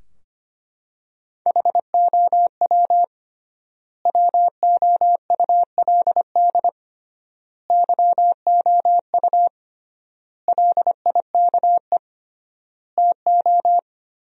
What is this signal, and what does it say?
Morse code, heard as a keyed tone, about 25 wpm, 700 Hz. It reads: HOW WOULD YOU LIKE TO
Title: 3x Word-Spacing - Sentences from Top 100 Words - 25wpm